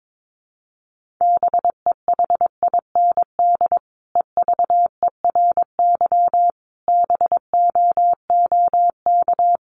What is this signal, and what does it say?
Morse code, heard as a keyed tone, about 22 wpm, 700 Hz. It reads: BEHIND EVERY BOOK
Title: Sentences from Top 500 Words 22wpm